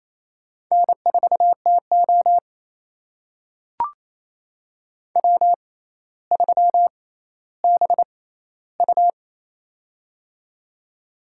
Morse code, heard as a keyed tone, about 29 wpm, spaced as for 15 wpm, 700 Hz. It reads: N4TO W3BU
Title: USA Call Signs - 4 Characters - 6x Character Spacing - 28wpm